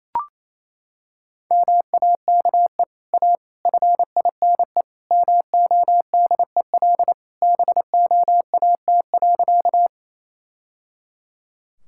Morse code, heard as a keyed tone, about 28 wpm, 700 Hz. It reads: MAKE A FINE MODEL BOAT.
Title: Sentences from Top 1000 Words - Review Part 5 - 28wpm